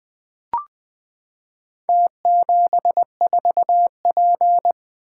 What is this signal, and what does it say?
Morse code, heard as a keyed tone, about 21 wpm, 700 Hz. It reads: T74P
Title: Four Letter-Number Combinations - 20wpm